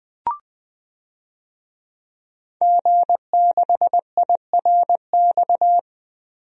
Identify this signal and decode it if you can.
Morse code, heard as a keyed tone, about 20 wpm, 700 Hz. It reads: G6IRX
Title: DXCC-100 Call Signs - 5 Characters - 20wpm